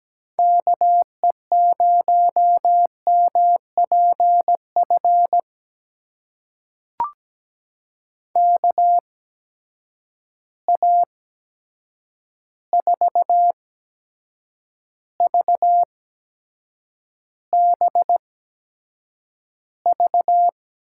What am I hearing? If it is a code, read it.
Morse code, heard as a keyed tone, about 17 wpm, spaced as for 7 wpm, 700 Hz. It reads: KE0MPF KA4VBV
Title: USA Call Signs - 6 Characters - 8x Character Spacing - 17wpm